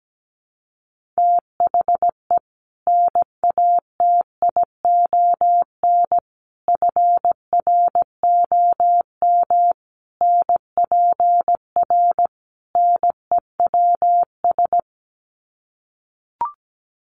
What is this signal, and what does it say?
Morse code, heard as a keyed tone, about 17 wpm, 700 Hz. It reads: THE NATION FROM NPR NEWS
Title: Sets of 5 Words 17wpm